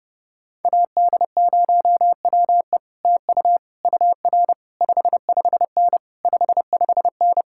Morse code, heard as a keyed tone, about 31 wpm, 700 Hz. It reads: AD0WE TU UR 55N 55N